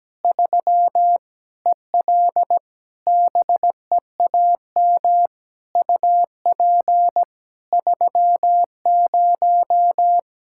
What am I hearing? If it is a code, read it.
Morse code, heard as a keyed tone, about 17 wpm, 700 Hz. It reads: 3 EL BEAM UP 30
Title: Practice QSO - Exchange 2 - Info - 17wpm